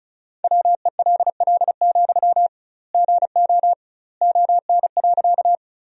Morse code, heard as a keyed tone, about 34 wpm, 700 Hz. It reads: WELL, GO ON.
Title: 2x Word-Spacing - Sentences from Top 200 Words - 35wpm